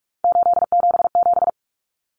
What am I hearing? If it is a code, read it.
Morse code, heard as a keyed tone, about 44 wpm, 700 Hz. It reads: 877
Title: Common Cut Numbers - 3-Digit 45wpm